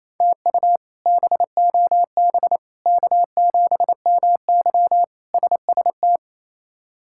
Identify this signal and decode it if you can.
Morse code, heard as a keyed tone, about 27 wpm, 700 Hz. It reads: TU BOB K7MY SST
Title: K1USN SST QSO Practice - 28wpm